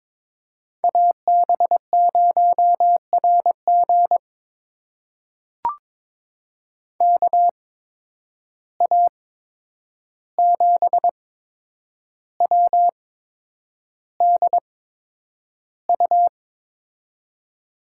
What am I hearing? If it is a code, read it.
Morse code, heard as a keyed tone, about 22 wpm, spaced as for 9 wpm, 700 Hz. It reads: AB0RG KA7WDU
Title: USA Call Signs - 8x Character Spacing - 22wpm